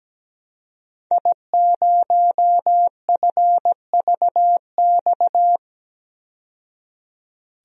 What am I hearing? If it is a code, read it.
Morse code, heard as a keyed tone, about 17 wpm, 700 Hz. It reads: I0FVX
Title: DXCC-100 Call Signs - 5 Characters - 17wpm